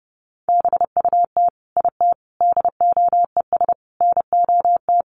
Morse code, heard as a keyed tone, about 30 wpm, 700 Hz. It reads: BUT IT DOES NOT
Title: Sentences from Top 1200 Words 30wpm